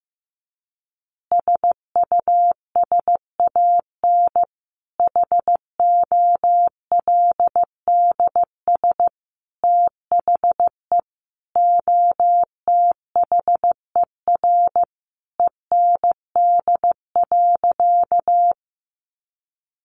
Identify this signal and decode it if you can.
Morse code, heard as a keyed tone, about 15 wpm, 700 Hz. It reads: SUSAN HOLDS THE OTHER END.
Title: Easy Phrases 15wpm